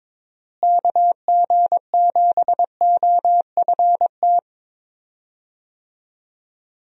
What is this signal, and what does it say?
Morse code, heard as a keyed tone, about 22 wpm, 700 Hz. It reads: KG7OFT
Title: USA Call Signs - 6 Characters - Repeated 3x - 22wpm